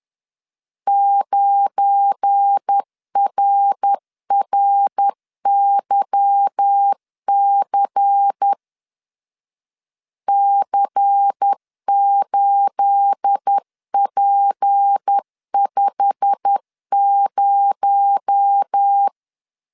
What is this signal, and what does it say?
Morse code, heard as a keyed tone, about 10 wpm, 800 Hz. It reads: 9RRYC C8P50
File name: CW / DET / 1 9 1500C S A 10wpm 4511